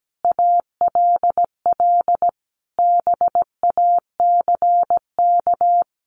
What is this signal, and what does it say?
Morse code, heard as a keyed tone, about 17 wpm, 700 Hz. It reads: ALL BACK
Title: Sentences from Top 1500 Words 17wpm